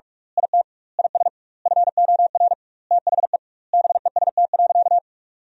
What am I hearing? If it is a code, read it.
Morse code, heard as a keyed tone, about 46 wpm, 700 Hz. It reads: IT IS FOR THE BEST.
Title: 2x Word-Spacing - Sentences from Top 500 Words - 45wpm